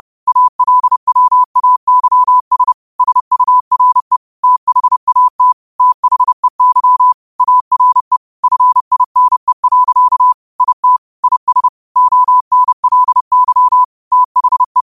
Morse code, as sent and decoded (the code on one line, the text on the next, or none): .- .-.. .-- .- -.-- ... / ... ..- .-. / - .... .- - / - .... -.-- / .- .-. / ..-. .. -. .-.-.- / .. - / .. ... / --- -. .-.. -.-- / - .... .
ALWAYS SURE THAT THEY ARE FINE. IT IS ONLY THE